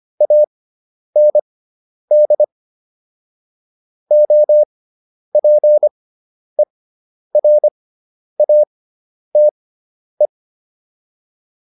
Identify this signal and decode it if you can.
Morse code, heard as a keyed tone, about 25 wpm, spaced as for 10 wpm, 600 Hz. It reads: AND OPERATE